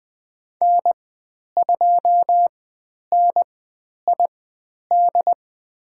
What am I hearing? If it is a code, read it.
Morse code, heard as a keyed tone, about 20 wpm, spaced as for 10 wpm, 700 Hz. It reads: N2NID
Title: 095 Lesson 16g - Letter c - Call Signs 20wpm